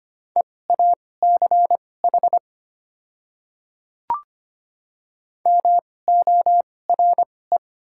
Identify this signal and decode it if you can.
Morse code, heard as a keyed tone, about 25 wpm, spaced as for 18 wpm, 700 Hz. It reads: EACH MORE
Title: 2x Character Spacing - Top 200 Words - 25wpm